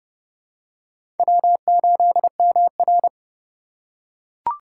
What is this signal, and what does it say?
Morse code, heard as a keyed tone, about 32 wpm, 700 Hz. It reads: W8MR
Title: USA Call Signs - 4 Characters - 4x Character Spacing - 30wpm